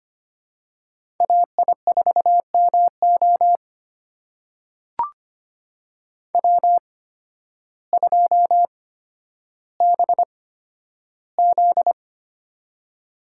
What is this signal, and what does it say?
Morse code, heard as a keyed tone, about 25 wpm, spaced as for 12 wpm, 700 Hz. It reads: AI4MO W2BZ
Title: USA Call Signs - 8x Character Spacing - 25wpm